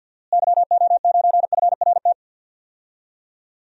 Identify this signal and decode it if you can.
Morse code, heard as a keyed tone, about 51 wpm, 700 Hz. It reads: YO9FRT
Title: DXCC-100 Call Signs - 6 Characters - Repeated 3x - 50wpm